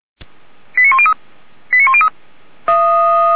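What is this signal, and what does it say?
Five-tone selective call (ZVEI), heard as quick runs of five tones, tones of about 80 ms. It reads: 89103 89103
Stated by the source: ZVEI five-tone call 89135 received by SDR, from the feuerwehr-orlamuende.org audio archive.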